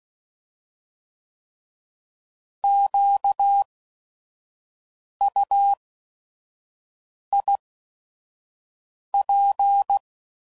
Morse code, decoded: QUIP

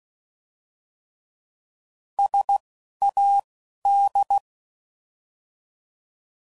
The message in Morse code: SAD

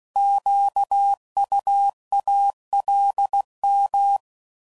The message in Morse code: QUALM

